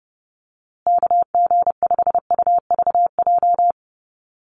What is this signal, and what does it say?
Morse code, heard as a keyed tone, about 30 wpm, 700 Hz. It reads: KG5UVJ
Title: USA Call Signs Repeated 3x 30wpm